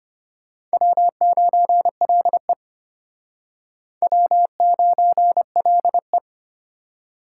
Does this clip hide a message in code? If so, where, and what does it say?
Yes: Morse code from 0.7 to 6.2 s, heard as a keyed tone, about 32 wpm, 700 Hz. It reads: W9LE W9LE